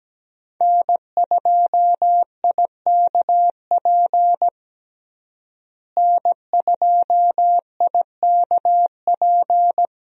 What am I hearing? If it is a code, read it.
Morse code, heard as a keyed tone, about 17 wpm, 700 Hz. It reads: N2IKP N2IKP
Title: USA Call Signs - 5 Characters - Repeated 3x - 17wpm